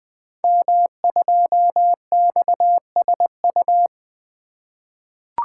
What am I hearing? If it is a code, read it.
Morse code, heard as a keyed tone, about 20 wpm, 700 Hz. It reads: M2XSU
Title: Code Groups - 5 Letter:Numbers - 20wpm